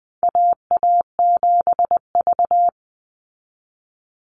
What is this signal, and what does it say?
Morse code, heard as a keyed tone, about 20 wpm, 700 Hz. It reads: AA7V
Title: Call Signs Repeated 2x - 20wpm